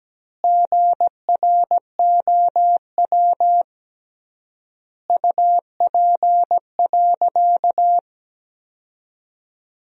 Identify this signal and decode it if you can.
Morse code, heard as a keyed tone, about 17 wpm, 700 Hz. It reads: GROW UP.
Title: 3x Word-Spacing - Sentences from Top 300 Words - 17wpm